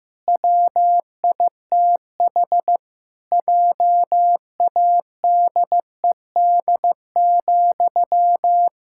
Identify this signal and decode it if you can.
Morse code, heard as a keyed tone, about 15 wpm, 700 Hz. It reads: WITH JADED,
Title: Pangram Challenge v2 - 15wpm